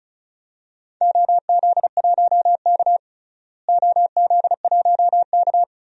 Morse code, heard as a keyed tone, about 36 wpm, 700 Hz. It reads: OZ1K OZ1K
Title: DXCC-100 Call Signs - 4 Characters - Repeated 2x - 35wpm